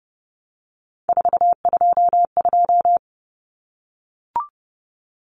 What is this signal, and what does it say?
Morse code, heard as a keyed tone, about 30 wpm, 700 Hz. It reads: 422
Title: Common Cut Numbers - 3-Digit 30wpm